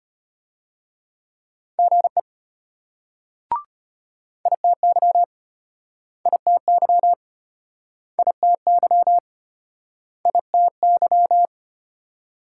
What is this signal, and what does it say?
Morse code, heard as a keyed tone, about 33 wpm, 700 Hz. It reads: GE ITY ITY ITY ITY